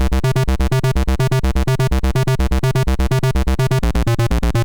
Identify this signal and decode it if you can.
There is no code.